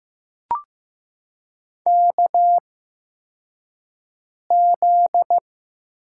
Morse code, heard as a keyed tone, about 15 wpm, spaced as for 4 wpm, 700 Hz. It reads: KZ
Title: USA Call Signs - 4 Characters - 8x Character Spacing - 15wpm